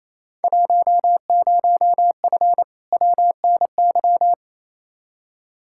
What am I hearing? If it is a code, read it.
Morse code, heard as a keyed tone, about 28 wpm, 700 Hz. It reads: 10F WNY
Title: Field Day Exchanges Encore - 28wpm